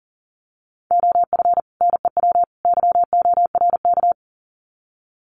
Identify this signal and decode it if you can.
Morse code, heard as a keyed tone, about 40 wpm, 700 Hz. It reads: OF NEW YORK
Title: Sets of 3 Words 40wpm